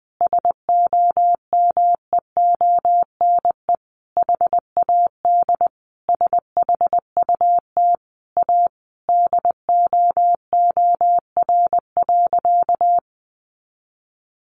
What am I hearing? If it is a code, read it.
Morse code, heard as a keyed tone, about 19 wpm, 700 Hz. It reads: SOMEONE HAD SHUT A DOOR.
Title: A Sound Of Thunder 20wpm